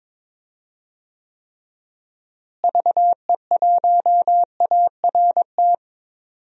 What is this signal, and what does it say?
Morse code, heard as a keyed tone, about 21 wpm, 700 Hz. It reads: VE1ART
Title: DXCC-100 Call Signs - 6 Characters - Repeated 3x - 22wpm